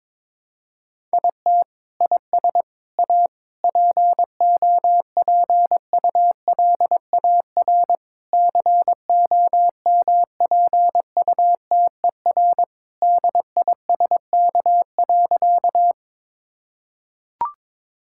Morse code, heard as a keyed tone, about 22 wpm, 700 Hz. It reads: IT IS A POPULAR COMPUTER DISK.
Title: Sentences from Top 1000 Words - Review Part 6 - 22wpm